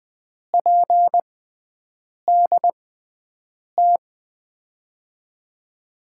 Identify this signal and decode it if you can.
Morse code, heard as a keyed tone, about 20 wpm, spaced as for 7 wpm, 700 Hz. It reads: PDT
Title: USA Call Signs - 6 Characters - 6x Character Spacing - 20wpm